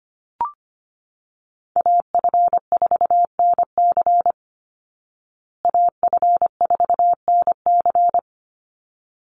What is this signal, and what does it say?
Morse code, heard as a keyed tone, about 25 wpm, 700 Hz. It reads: AF4NC AF4NC
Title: Call Signs Repeated 3x - 25wpm